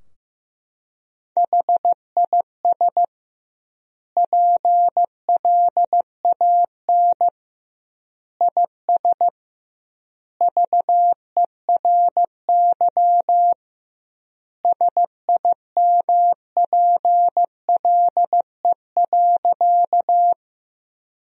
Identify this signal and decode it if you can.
Morse code, heard as a keyed tone, about 15 wpm, 700 Hz. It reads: HIS PLAN IS VERY SIMPLE.
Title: 2x Word-Spacing - Sentences from Top 500 Words - 15wpm